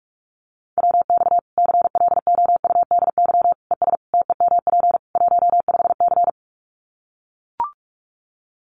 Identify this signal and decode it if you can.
Morse code, heard as a keyed tone, about 44 wpm, 700 Hz. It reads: WX CLOUDY ES TEMP 15C
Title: Common QSO Phrases 45wpm